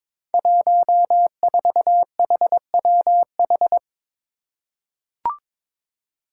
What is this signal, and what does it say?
Morse code, heard as a keyed tone, about 22 wpm, 700 Hz. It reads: 14HWH
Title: Five Challenging Letter-Number Combinations 22wpm